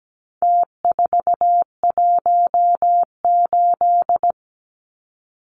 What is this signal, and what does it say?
Morse code, heard as a keyed tone, about 17 wpm, 700 Hz. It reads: T418
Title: Numbers - Contest Style - 4-Digit 17wpm